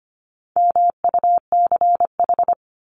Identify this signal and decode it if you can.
Morse code, heard as a keyed tone, about 25 wpm, 700 Hz. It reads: MUCH
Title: Top 300 Words 25wpm